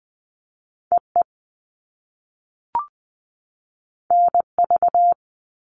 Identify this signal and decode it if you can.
Morse code, heard as a keyed tone, about 20 wpm, 700 Hz. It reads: E E N V
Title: All Cut Numbers - 2-Digit 20wpm